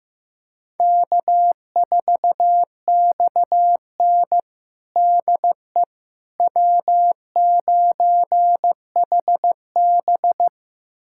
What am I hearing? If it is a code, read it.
Morse code, heard as a keyed tone, about 15 wpm, 700 Hz. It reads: K4XN DE W9HB
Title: USA Callsign DE Callsign - 4 Characters - 15wpm